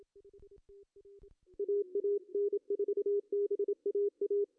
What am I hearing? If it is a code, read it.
Morse code, heard as a keyed tone, about 25 wpm, 400 Hz. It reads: AA N4BAA
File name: CW / cqww-cw-k5tr